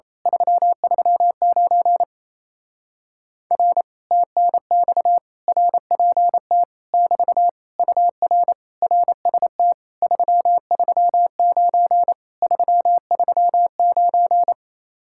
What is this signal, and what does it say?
Morse code, heard as a keyed tone, about 26 wpm, 700 Hz. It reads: 339 R TNX RPT <BT> UR RST 339 339
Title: Speed-Racing Common QSO Phrases - 28wpm